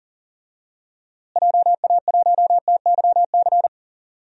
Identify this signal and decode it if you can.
Morse code, heard as a keyed tone, about 39 wpm, 700 Hz. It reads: JA1TYC
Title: DXCC-100 Call Signs - 6 Characters - Repeated 2x - 40wpm